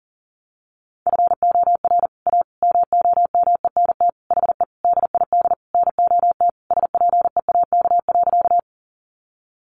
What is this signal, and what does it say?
Morse code, heard as a keyed tone, about 40 wpm, 700 Hz. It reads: FOR A MOMENT HE DID NOT SPEAK.
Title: Sentences from Top 1100 Words 40wpm